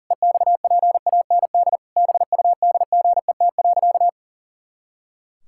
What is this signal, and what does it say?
Morse code, heard as a keyed tone, about 40 wpm, 700 Hz. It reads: EXPAND BUDGET.
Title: Pangram Challenge v2 - 40wpm